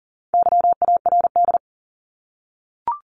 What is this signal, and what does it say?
Morse code, heard as a keyed tone, about 40 wpm, 700 Hz. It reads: YARD
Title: Top 1751-2000 Words 40wpm